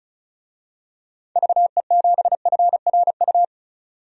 Morse code, heard as a keyed tone, about 34 wpm, 700 Hz. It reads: VE7FRU
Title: DXCC-100 Call Signs - 6 Characters - 35wpm